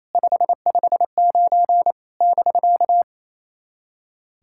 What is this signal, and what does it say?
Morse code, heard as a keyed tone, about 28 wpm, 700 Hz. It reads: 559 <BK>
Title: SOTA QSO Practice 28wpm